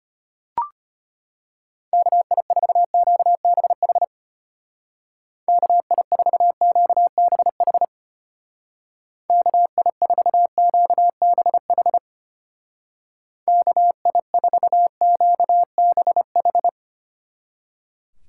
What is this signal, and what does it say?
Morse code, heard as a keyed tone, about 31 wpm, 700 Hz. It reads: KI4QBH KI4QBH KI4QBH KI4QBH
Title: Speed Racing - USA 2x3 Call Signs 25wpm